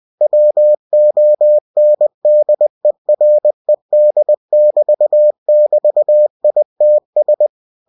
Morse code, heard as a keyed tone, about 20 wpm, 600 Hz. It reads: WONDERED<BT><BT>ITS
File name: cw-books / winnie-the-pooh / winnie-20wpm-page-0084